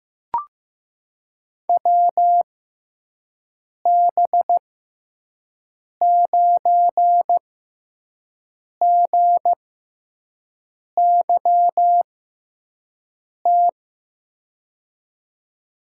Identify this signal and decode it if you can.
Morse code, heard as a keyed tone, about 15 wpm, spaced as for 5 wpm, 700 Hz. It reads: WB9GYT
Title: USA Call Signs - 6 Characters - 6x Character Spacing - 15wpm